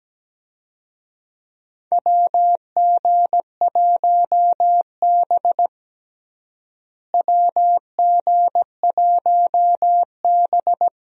Morse code, heard as a keyed tone, about 17 wpm, 700 Hz. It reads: WG1B WG1B
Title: USA Call Signs - 4 Characters - Repeated 2x - 17wpm